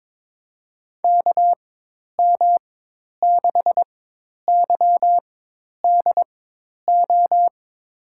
Morse code, heard as a keyed tone, about 22 wpm, spaced as for 10 wpm, 700 Hz. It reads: KM6YDO
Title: USA Call Signs - 6 Characters - 4x Character Spacing - 22wpm